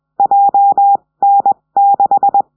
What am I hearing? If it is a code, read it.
Morse code, heard as a keyed tone, about 20 wpm, spaced as for 17 wpm, 800 Hz. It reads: JN6
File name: 70cm